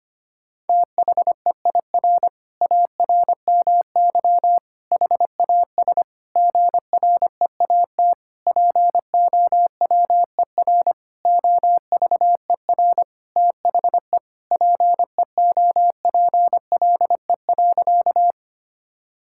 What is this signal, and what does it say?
Morse code, heard as a keyed tone, about 25 wpm, 700 Hz. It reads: THEIR ARMY HAS GREAT POWER OVER THE PEOPLE.
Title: Sentences from Top 1000 Words - Review Part 5 - 25wpm